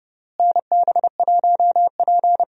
Morse code, heard as a keyed tone, about 30 wpm, 700 Hz. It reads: NB1P
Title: USA Call Signs - 4 Characters - Repeated 3x - 30wpm